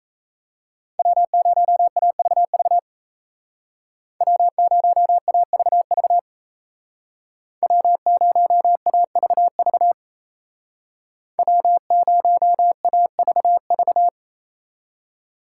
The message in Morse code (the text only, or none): W0AVV W0AVV W0AVV W0AVV